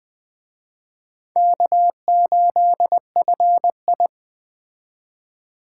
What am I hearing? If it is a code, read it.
Morse code, heard as a keyed tone, about 20 wpm, 700 Hz. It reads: K8FI